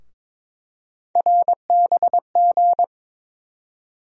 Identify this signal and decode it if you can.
Morse code, heard as a keyed tone, about 22 wpm, 700 Hz. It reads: RBG
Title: USA Call Sign Postfixes - 3 Letters 22wpm